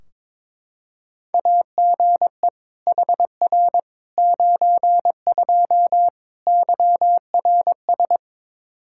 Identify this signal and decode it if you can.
Morse code, heard as a keyed tone, about 22 wpm, 700 Hz. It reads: AGE HR 92 YRS
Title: Practice QSO - Exchange 3 - More Info - 22wpm